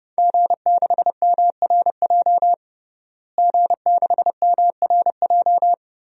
Morse code, heard as a keyed tone, about 30 wpm, 700 Hz. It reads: G6MRJ G6MRJ
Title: DXCC-100 Call Signs - 5 Characters - Repeated 3x - 30wpm